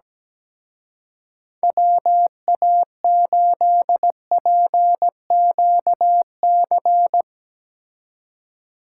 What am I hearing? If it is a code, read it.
Morse code, heard as a keyed tone, about 17 wpm, 700 Hz. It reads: WA8PQC
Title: Speed Racing - USA 2x3 Call Signs 17wpm